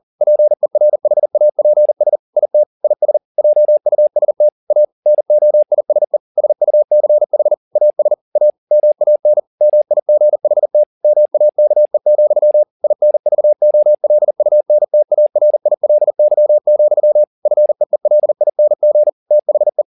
Morse code, heard as a keyed tone, about 40 wpm, 600 Hz. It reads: PERHAPS IT IS JUST A NOISE SUCH AS A MAN MIGHT MAKE, INVOLUNTARILY, FEELING THE
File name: the old man and the sea 2 40wpm